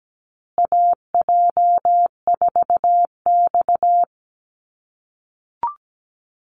Morse code, heard as a keyed tone, about 17 wpm, 700 Hz. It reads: AJ4X
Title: USA Call Signs Repeated 2x 17wpm